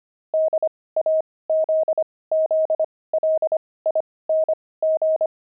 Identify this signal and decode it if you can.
Morse code, heard as a keyed tone, about 25 wpm, spaced as for 18 wpm, 630 Hz. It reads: DAZZLING
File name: pangram-fs-18-cs-25-jelly